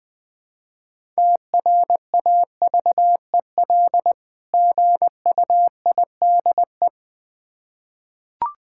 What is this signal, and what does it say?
Morse code, heard as a keyed tone, about 20 wpm, 700 Hz. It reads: TRAVEL GUIDE
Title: Difficult Words with Hints - 20wpm